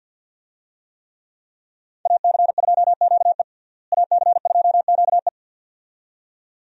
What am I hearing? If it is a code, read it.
Morse code, heard as a keyed tone, about 51 wpm, 700 Hz. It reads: AC2QE AC2QE
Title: USA Call Signs - 5 Characters - Repeated 2x - 50wpm